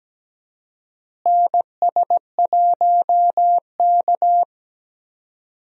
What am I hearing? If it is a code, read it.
Morse code, heard as a keyed tone, about 17 wpm, 700 Hz. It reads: NS1K
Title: USA Call Signs - 4 Characters - Repeated 3x - 17wpm